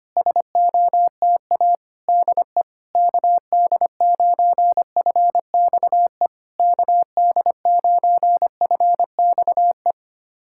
Morse code, heard as a keyed tone, about 25 wpm, 700 Hz. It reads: SOTA DE KD9FXE KD9FXE
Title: SOTA QSO Practice 25wpm